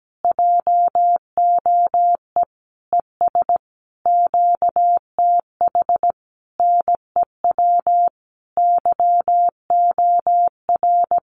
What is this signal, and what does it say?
Morse code, heard as a keyed tone, about 17 wpm, 700 Hz. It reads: JOE ES QTH NEW YOR
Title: Common QSO Phrases Encore 17wpm